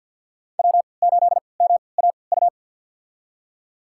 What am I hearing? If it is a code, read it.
Morse code, heard as a keyed tone, about 47 wpm, spaced as for 27 wpm, 700 Hz. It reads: W8MAU
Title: USA Call Signs - 3x Character Spacing - 50wpm